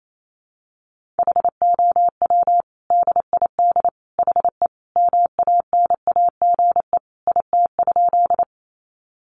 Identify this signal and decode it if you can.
Morse code, heard as a keyed tone, about 28 wpm, 700 Hz. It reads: HOW DID HE MANAGE IT?
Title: Sentences from Top 1400 Words 28wpm